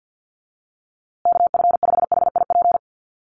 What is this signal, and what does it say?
Morse code, heard as a keyed tone, about 50 wpm, 700 Hz. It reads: KF5HIP
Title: USA Call Signs Repeated 3x 50wpm